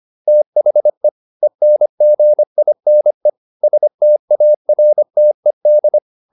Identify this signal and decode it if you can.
Morse code, heard as a keyed tone, about 25 wpm, 600 Hz. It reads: THE ENGINE STARTED